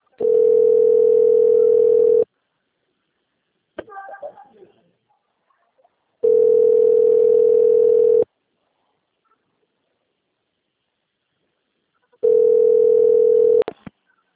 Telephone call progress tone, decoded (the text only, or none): ringback tone